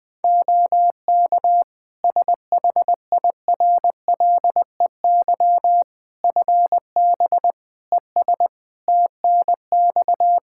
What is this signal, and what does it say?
Morse code, heard as a keyed tone, about 19 wpm, 700 Hz. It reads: OK SHIRLEY FB ES TNX
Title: Practice QSO - Exchange 2 - Info - 20wpm